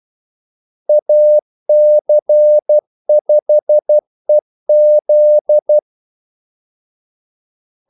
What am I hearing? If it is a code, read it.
Morse code, heard as a keyed tone, about 12 wpm, 600 Hz. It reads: AC5EZ
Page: 2 ekstra ordmellemrum mellem kaldesignalerne